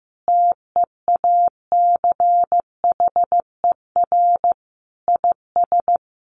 Morse code, heard as a keyed tone, about 15 wpm, 700 Hz. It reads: TEACHER IS
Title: Additional Easy Phrases 15wpm